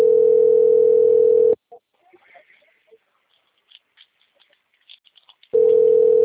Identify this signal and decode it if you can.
Ringback tone (North America), 440 + 480 Hz.